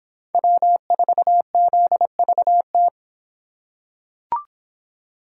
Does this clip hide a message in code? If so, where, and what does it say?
Yes: Morse code from 0.3 to 2.9 s, heard as a keyed tone, about 25 wpm, 700 Hz. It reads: W4ZVT